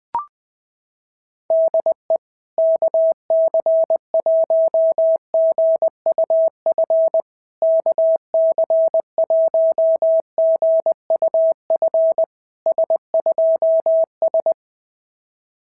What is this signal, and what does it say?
Morse code, heard as a keyed tone, about 20 wpm, 660 Hz. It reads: DE KC1GUF KC1GUF S2S
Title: SOTA QSO Practice 20wpm